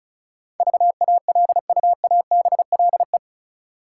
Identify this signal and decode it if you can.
Morse code, heard as a keyed tone, about 34 wpm, 700 Hz. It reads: VALUABLE
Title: Sentences from Top 1100 Words - New Word Drill Alt - 35wpm